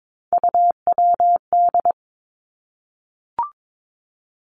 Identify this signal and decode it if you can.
Morse code, heard as a keyed tone, about 22 wpm, 700 Hz. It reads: UWD
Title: Three Challenging Letter-Number Combinations 22wpm